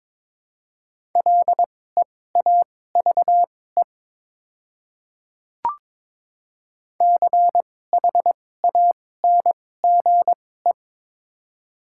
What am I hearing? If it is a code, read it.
Morse code, heard as a keyed tone, about 22 wpm, spaced as for 16 wpm, 690 Hz. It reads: LEAVE CHANGE
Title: 2x Character Spacing - Top 500 Words - 22wpm